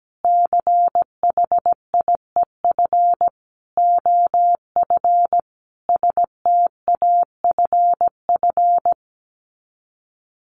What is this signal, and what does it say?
Morse code, heard as a keyed tone, about 17 wpm, 700 Hz. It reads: CHIEF OF STAFF
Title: Sets of 5 Words Encore 17wpm